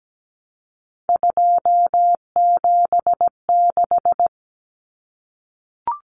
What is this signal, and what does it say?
Morse code, heard as a keyed tone, about 17 wpm, 700 Hz. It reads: 276
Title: Numbers - 3-Digit 17wpm